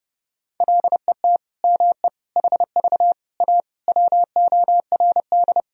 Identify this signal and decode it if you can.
Morse code, heard as a keyed tone, about 29 wpm, 700 Hz. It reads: LET ME HV A WORD
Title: Sentences from Top 300 Words and Abbreviations 30wpm